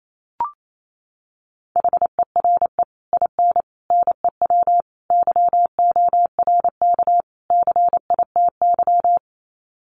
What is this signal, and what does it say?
Morse code, heard as a keyed tone, about 28 wpm, 700 Hz. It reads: HERE IN NEW YORK CITY
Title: Sets of 5 Words Encore2 28wpm